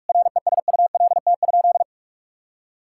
Morse code, heard as a keyed tone, about 43 wpm, 700 Hz. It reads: RESULT?
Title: Sentences from Top 1000 Words - Review Part 5 - 45wpm